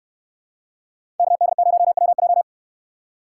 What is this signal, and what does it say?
Morse code, heard as a keyed tone, about 69 wpm, 700 Hz. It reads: KD0WJ